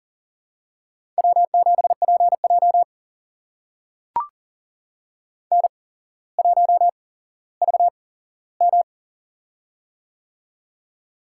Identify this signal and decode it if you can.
Morse code, heard as a keyed tone, about 40 wpm, spaced as for 17 wpm, 700 Hz. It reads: W7PJ N1VM
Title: USA Call Signs - 4 Characters - 8x Character Spacing - 40wpm